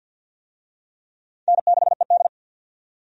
Morse code, heard as a keyed tone, about 51 wpm, 700 Hz. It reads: N6ED
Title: USA Call Signs - 4 Characters - 50wpm